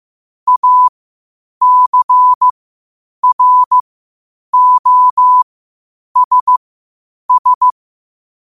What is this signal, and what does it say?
Morse code, heard as a keyed tone, about 15 wpm, spaced as for 8 wpm, 1000 Hz. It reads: ACROSS
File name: qotdfiles / 8wpm